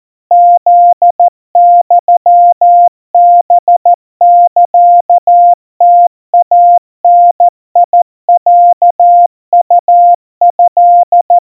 Morse code (Japanese Wabun code), heard as a keyed tone, about 14 wpm, 700 Hz. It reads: フユハサムイダロウト
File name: kawa / Morse sample / kamyu 50 cpm 700 Hz Crispy